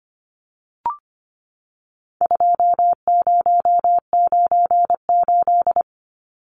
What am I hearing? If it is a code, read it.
Morse code, heard as a keyed tone, about 25 wpm, 700 Hz. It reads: 2098
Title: Numbers - 4-Digit 25wpm